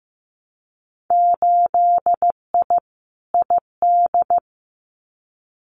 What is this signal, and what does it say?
Morse code, heard as a keyed tone, about 15 wpm, 700 Hz. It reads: 8I ID
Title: Winter Field Day Exchanges Encore 15wpm